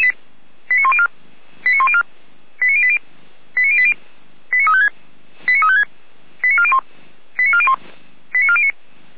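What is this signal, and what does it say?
Five-tone selective call (ZVEI), heard as quick runs of five tones, tones of about 70 ms. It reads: 89104 89104 89080 89080 89356 89356 89401 89401 89409
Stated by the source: Received by SDR as ZVEI five-tone selective call 89350.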